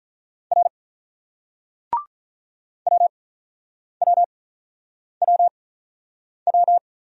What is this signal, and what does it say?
Morse code, heard as a keyed tone, about 44 wpm, 700 Hz. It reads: R W W W W
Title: Speed-Racing Single Letters 35wpm